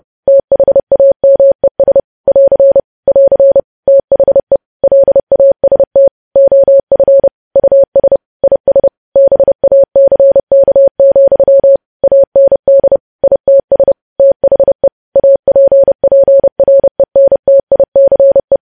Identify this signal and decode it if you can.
Morse code, heard as a keyed tone, about 30 wpm, 550 Hz. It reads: THAMES <AR> <AR> THE LAST OF US IS BACK, AND ITS THE APPRENTICE
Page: news headlines in morse code at 30 wpm